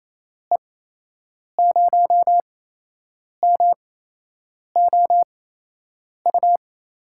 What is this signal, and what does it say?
Morse code, heard as a keyed tone, about 28 wpm, spaced as for 8 wpm, 700 Hz. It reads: E0MOU
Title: USA Call Signs - 6 Characters - 8x Character Spacing - 28wpm